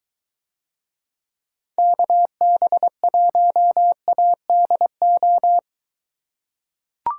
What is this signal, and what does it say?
Morse code, heard as a keyed tone, about 23 wpm, 700 Hz. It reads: KB1ADO